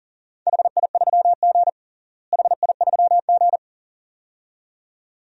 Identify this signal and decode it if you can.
Morse code, heard as a keyed tone, about 40 wpm, 700 Hz. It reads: HI3G HI3G
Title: DXCC-100 Call Signs - 4 Characters - Repeated 2x - 40wpm